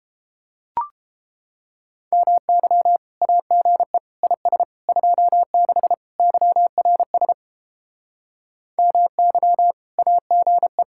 Morse code, heard as a keyed tone, about 31 wpm, 700 Hz. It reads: MY AGE IS 26 YRS MY AGE